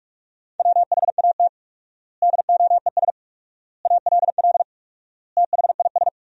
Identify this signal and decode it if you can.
Morse code, heard as a keyed tone, about 46 wpm, 700 Hz. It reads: WHAT DOES ALL THIS
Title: 4x Word-Spacing - Sentences from Top 200 Words - 45wpm